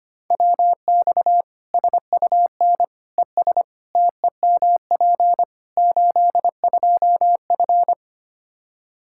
Morse code, heard as a keyed tone, about 25 wpm, 700 Hz. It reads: WX SUN ES TEMP 82F
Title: K1USN SST QSO Practice 25wpm